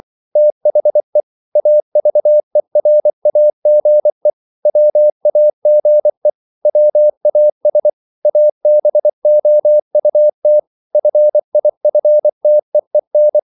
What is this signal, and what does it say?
Morse code, heard as a keyed tone, about 24 wpm, 600 Hz. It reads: THE AVERAGE WAGE WAS ABOUT FIFTEEN